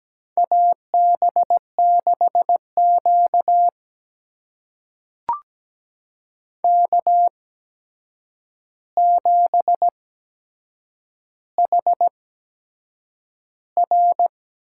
Morse code, heard as a keyed tone, about 17 wpm, spaced as for 7 wpm, 700 Hz. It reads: AB6Q K7HR
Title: USA Call Signs - 8x Character Spacing - 17wpm